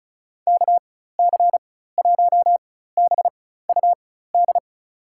Morse code, heard as a keyed tone, about 34 wpm, spaced as for 16 wpm, 700 Hz. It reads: KC1BUD
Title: USA Call Signs - 4x Character Spacing - 35wpm